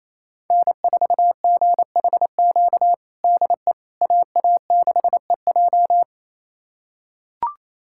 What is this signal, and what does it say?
Morse code, heard as a keyed tone, about 28 wpm, 700 Hz. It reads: N4GHQ DE AA6EJ
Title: USA Callsign DE Callsign - 5 Characters - 28wpm